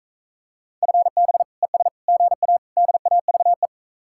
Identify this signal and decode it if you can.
Morse code, heard as a keyed tone, about 44 wpm, 700 Hz. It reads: FB ES GA DAVE